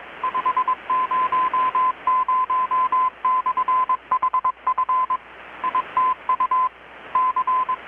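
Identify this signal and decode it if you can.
Morse code, heard as a keyed tone, about 22 wpm, 1020 Hz. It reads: 500/HF ITU C
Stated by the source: Morse from KPH's Night of Nights XX maritime radiotelegraphy event, received in Chiapas, Mexico about 3,800 km away on a Kenwood R-600 with a grounded 10 m long-wire antenna.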